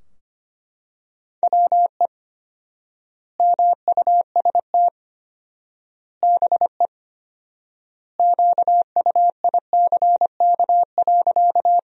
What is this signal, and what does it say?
Morse code, heard as a keyed tone, about 25 wpm, 700 Hz. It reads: WE MUST BE QUICK.
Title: 4x Word-Spacing - Sentences from Top 500 Words - 25wpm